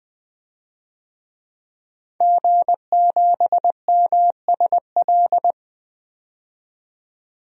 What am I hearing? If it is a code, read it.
Morse code, heard as a keyed tone, about 20 wpm, 700 Hz. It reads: G7MSL
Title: DXCC-100 Call Signs - 5 Characters - 20wpm